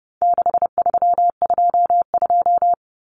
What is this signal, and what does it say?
Morse code, heard as a keyed tone, about 30 wpm, 700 Hz. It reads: 6322
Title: Numbers - 4-Digit 30wpm